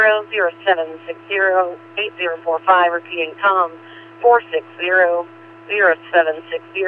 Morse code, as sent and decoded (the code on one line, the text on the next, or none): none